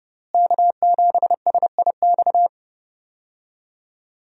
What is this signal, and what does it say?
Morse code, heard as a keyed tone, about 30 wpm, 700 Hz. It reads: K7SIX
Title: USA Call Signs - 5 Characters - Repeated 2x - 30wpm